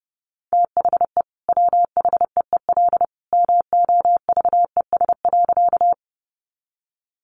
Morse code, heard as a keyed tone, about 30 wpm, 700 Hz. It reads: THE WHEEL MOVES.
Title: Additional Easy Phrases 30wpm